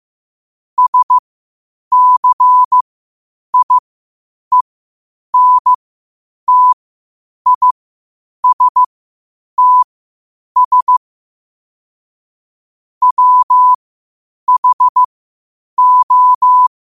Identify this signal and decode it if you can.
Morse code, heard as a keyed tone, about 15 wpm, spaced as for 8 wpm, 1000 Hz. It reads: SCIENTISTS WHO
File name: qotdfiles / 8wpm